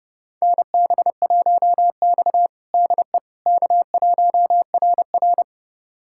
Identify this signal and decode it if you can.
Morse code, heard as a keyed tone, about 28 wpm, 700 Hz. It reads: NB1X DE K1RR
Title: USA Callsign DE Callsign - 4 Characters - 30wpm